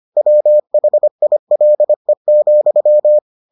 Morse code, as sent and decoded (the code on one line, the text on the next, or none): .-- .... .. .-.. --..--
WHILE,